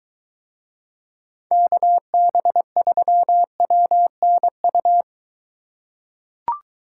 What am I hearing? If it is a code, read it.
Morse code, heard as a keyed tone, about 21 wpm, 700 Hz. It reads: KB3WNU